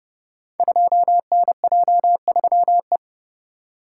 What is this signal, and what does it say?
Morse code, heard as a keyed tone, about 30 wpm, 700 Hz. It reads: 2NJ3E
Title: Code Groups - 5 Letter:Numbers - 30wpm